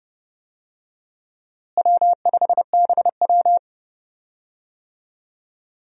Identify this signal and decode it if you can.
Morse code, heard as a keyed tone, about 31 wpm, 700 Hz. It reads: W5BW
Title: USA Call Signs - 4 Characters - Repeated 2x - 30wpm